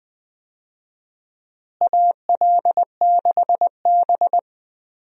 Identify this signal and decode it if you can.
Morse code, heard as a keyed tone, about 20 wpm, 700 Hz. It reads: AL6B
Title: USA Call Signs - 4 Characters - Repeated 3x - 20wpm